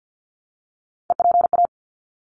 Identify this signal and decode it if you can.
Morse code, heard as a keyed tone, about 50 wpm, 700 Hz. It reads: EPA
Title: ARRL Sections 50wpm